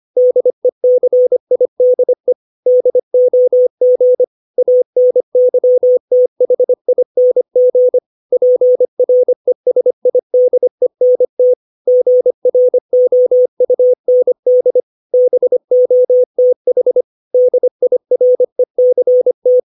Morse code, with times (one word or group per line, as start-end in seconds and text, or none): 0.2-2.3 DECIDE
2.7-4.2 DOG
4.6-8.0 ANYTHING
8.3-11.5 PRESIDENT
11.9-14.8 GROUND
15.1-17.0 BOTH
17.3-19.6 DIRECT